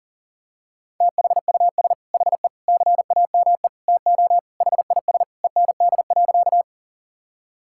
Morse code, heard as a keyed tone, about 40 wpm, 700 Hz. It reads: THUS HE CAME TO HIS END.
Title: Sentences from Top 1000 Words - Review Part 5 - 40wpm